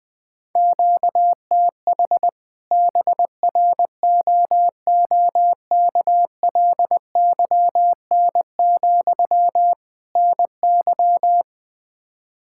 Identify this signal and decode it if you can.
Morse code, heard as a keyed tone, about 21 wpm, 700 Hz. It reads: QTH BROOKLYN, NY
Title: Practice QSO - Exchange 1 - Beginning - 20wpm